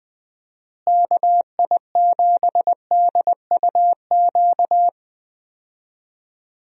Morse code, heard as a keyed tone, about 20 wpm, 700 Hz. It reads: KI7DUQ
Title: USA Call Signs - 6 Characters - Repeated 3x - 20wpm